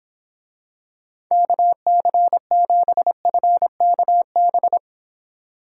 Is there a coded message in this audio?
Yes: Morse code from 1.3 to 4.8 s, heard as a keyed tone, about 25 wpm, 700 Hz. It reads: KC7FKB